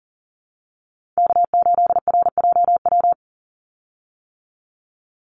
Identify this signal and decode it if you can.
Morse code, heard as a keyed tone, about 40 wpm, 700 Hz. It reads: K8RJW
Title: Call Signs - 40wpm